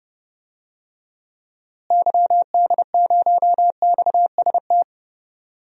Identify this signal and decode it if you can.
Morse code, heard as a keyed tone, about 30 wpm, 700 Hz. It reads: YD0XST
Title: DXCC-100 Call Signs - 6 Characters - Repeated 3x - 30wpm